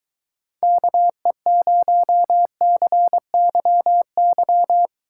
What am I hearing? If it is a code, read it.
Morse code, heard as a keyed tone, about 22 wpm, 700 Hz. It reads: KE0CYY